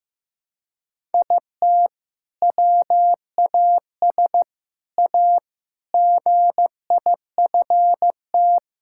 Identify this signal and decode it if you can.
Morse code, heard as a keyed tone, about 15 wpm, 700 Hz. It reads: IT WAS A GIFT
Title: Sentences from Top 1000 Words - Review Part 6 - 15wpm